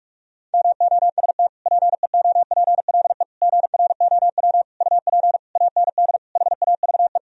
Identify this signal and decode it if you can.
Morse code, heard as a keyed tone, about 46 wpm, 700 Hz. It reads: MOST PEOPLE GROW UP AND HAVE
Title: Sentences from Top 1000 Words - Review Part 1 - 45wpm